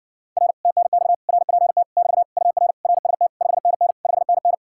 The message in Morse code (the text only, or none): R TNX RPT <BT> UR RST 5NN 5NN